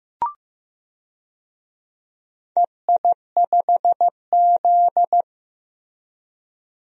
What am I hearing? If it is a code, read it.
Morse code, heard as a keyed tone, about 15 wpm, 700 Hz. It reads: EI5Z
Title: DXCC-100 Call Signs - 4 Characters - Repeated 3x - 15wpm